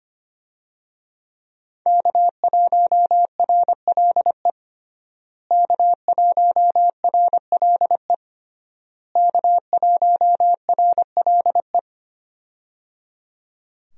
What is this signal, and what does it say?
Morse code, heard as a keyed tone, about 25 wpm, 700 Hz. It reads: K1RLE K1RLE K1RLE
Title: USA Call Signs - 5 Characters - Repeated 3x - 25wpm